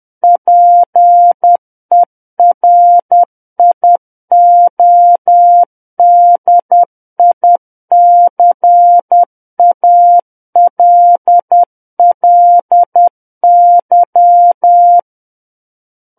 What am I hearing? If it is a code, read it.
Morse code, heard as a keyed tone, about 10 wpm, 700 Hz. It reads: PERIODICALLY
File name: Book One / 01-Chapter 1 - 10 WPM